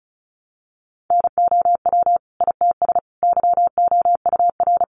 Morse code, heard as a keyed tone, about 34 wpm, 700 Hz. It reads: NOW ITS YOUR
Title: Sentences from Top 600 Words 35wpm